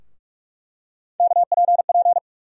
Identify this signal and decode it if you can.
Morse code, heard as a keyed tone, about 46 wpm, 700 Hz. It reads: KPP
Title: USA Call Sign Postfixes - 3 Letters 45wpm